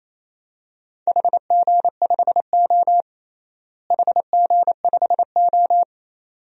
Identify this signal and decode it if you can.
Morse code, heard as a keyed tone, about 28 wpm, 700 Hz. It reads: HG5O HG5O
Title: DXCC-100 Call Signs - 4 Characters - Repeated 2x - 28wpm